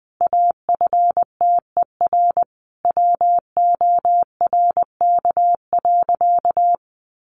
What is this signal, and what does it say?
Morse code, heard as a keyed tone, about 20 wpm, 700 Hz. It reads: AFTER WORK.
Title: Sentences from Top 1400 Words 20wpm